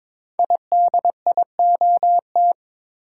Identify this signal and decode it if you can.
Morse code, heard as a keyed tone, about 22 wpm, 690 Hz. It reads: IDIOT
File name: 5 Letter Words - First 3 Letters Given - 22wpm